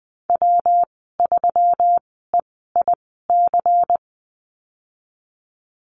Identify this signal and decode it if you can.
Morse code, heard as a keyed tone, about 20 wpm, spaced as for 15 wpm, 700 Hz. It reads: W3EIC
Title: 179 Lesson 29g - Slash - Call Signs 20wpm